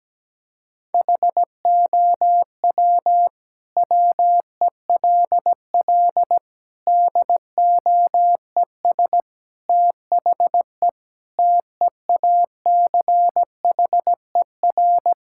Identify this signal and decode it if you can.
Morse code, heard as a keyed tone, about 17 wpm, 700 Hz. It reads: HOW WELL DOES THE TEACHER
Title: Sentences from Top 1000 Words - Review Part 6 - 17wpm